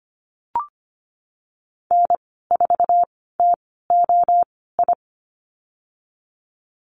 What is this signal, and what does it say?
Morse code, heard as a keyed tone, about 25 wpm, spaced as for 16 wpm, 700 Hz. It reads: N4TOI
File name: Lesson 15g - Number 5 - Call Signs 25wpm